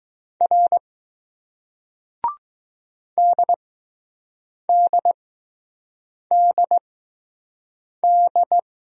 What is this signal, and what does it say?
Morse code, heard as a keyed tone, about 20 wpm, spaced as for 6 wpm, 700 Hz. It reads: R DDDD